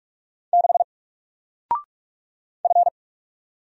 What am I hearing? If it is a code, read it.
Morse code, heard as a keyed tone, about 44 wpm, 700 Hz. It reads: 6 F
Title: Single Letter-Number Alt 45wpm